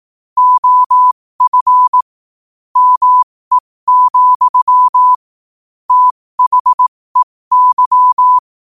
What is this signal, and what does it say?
Morse code, heard as a keyed tone, about 18 wpm, spaced as for 15 wpm, 1000 Hz. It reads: OF ME, THEY